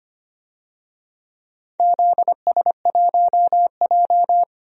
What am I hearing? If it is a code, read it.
Morse code, heard as a keyed tone, about 25 wpm, 700 Hz. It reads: ZS1J
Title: DXCC-100 Call Signs - 4 Characters - 25wpm